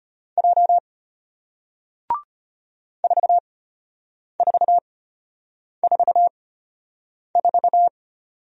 Morse code, heard as a keyed tone, about 34 wpm, spaced as for 8 wpm, 700 Hz. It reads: J 4444